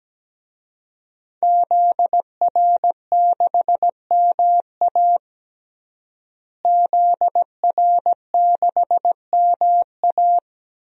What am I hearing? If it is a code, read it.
Morse code, heard as a keyed tone, about 17 wpm, 700 Hz. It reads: ZR6MA ZR6MA
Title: DXCC-100 Call Signs - 5 Characters - Repeated 3x - 17wpm